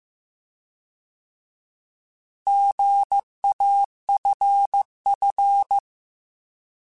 Morse code, heard as a keyed tone, about 15 wpm, 800 Hz. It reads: GAFF